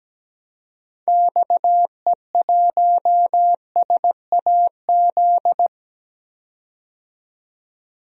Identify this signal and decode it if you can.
Morse code, heard as a keyed tone, about 17 wpm, 700 Hz. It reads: XE1SAZ
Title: DXCC-100 Call Signs - 6 Characters - 17wpm